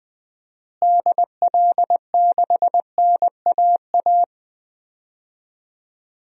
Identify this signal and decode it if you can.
Morse code, heard as a keyed tone, about 20 wpm, 700 Hz. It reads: DL6NAA
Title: DXCC-100 Call Signs - 6 Characters - Repeated 3x - 20wpm